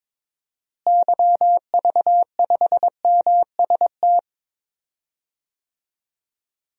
Morse code, heard as a keyed tone, about 22 wpm, 700 Hz. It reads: YV5MST
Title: DXCC-100 Call Signs - 6 Characters - Repeated 2x - 22wpm